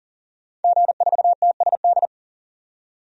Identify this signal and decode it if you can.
Morse code, heard as a keyed tone, about 40 wpm, 700 Hz. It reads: G4TSD
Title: DXCC-100 Call Signs - 5 Characters - 40wpm